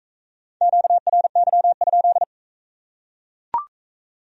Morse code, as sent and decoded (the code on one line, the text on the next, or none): --.- .-. -.-- ..--..
QRY?